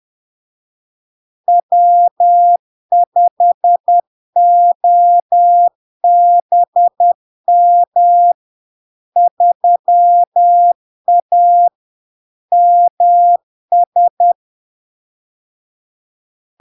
Morse code, heard as a keyed tone, about 10 wpm, 700 Hz. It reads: W5OBM 3A MS
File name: LARK-FD-2022-Log-in-CW-10wpm-700-Hz